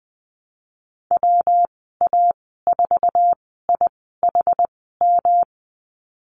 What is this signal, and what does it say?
Morse code, heard as a keyed tone, about 20 wpm, spaced as for 14 wpm, 700 Hz. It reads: WA4IHM
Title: 128 Lesson 21g - Number 6 - Call Signs 20wpm